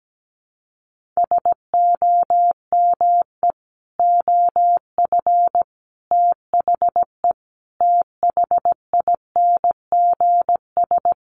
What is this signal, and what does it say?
Morse code, heard as a keyed tone, about 17 wpm, 700 Hz. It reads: SOME OF THE THINGS
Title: Sets of 5 Words 17wpm